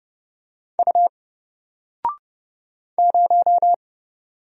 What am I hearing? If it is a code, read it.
Morse code, heard as a keyed tone, about 29 wpm, 700 Hz. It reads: U 0
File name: Single-Letter-Number-30wpm